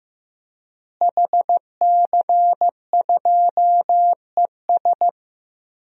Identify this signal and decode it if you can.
Morse code, heard as a keyed tone, about 16 wpm, 700 Hz. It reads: HC2ES
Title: DXCC-100 Call Signs - 5 Characters - Repeated 3x - 15wpm